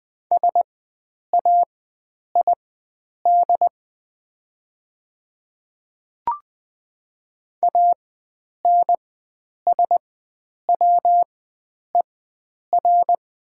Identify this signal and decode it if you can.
Morse code, heard as a keyed tone, about 20 wpm, spaced as for 9 wpm, 700 Hz. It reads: SAID ANSWER
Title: 4x Character Spacing - Top 500 Words - 20wpm